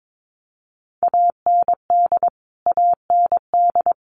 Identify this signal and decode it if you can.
Morse code, heard as a keyed tone, about 22 wpm, 700 Hz. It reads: AND AND
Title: Top 200 Words Repeated 22wpm